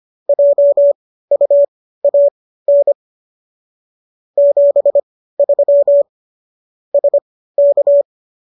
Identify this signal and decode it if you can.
Morse code, heard as a keyed tone, about 25 wpm, spaced as for 15 wpm, 570 Hz. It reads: JUAN 73 SK
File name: qso204 15